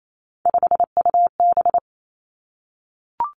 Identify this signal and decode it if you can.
Morse code, heard as a keyed tone, about 28 wpm, 700 Hz. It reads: 5UB
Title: Three Challenging Letter-Number Combinations 28wpm